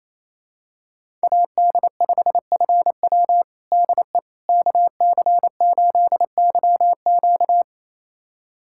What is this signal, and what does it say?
Morse code, heard as a keyed tone, about 27 wpm, 700 Hz. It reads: AD5FW DE KC8YQ
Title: USA Callsign DE Callsign - 5 Characters - 28wpm